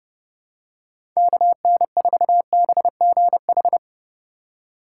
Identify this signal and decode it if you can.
Morse code, heard as a keyed tone, about 32 wpm, 700 Hz. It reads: KN4BGH
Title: USA Call Signs - 6 Characters - 2x Character Spacing - 30wpm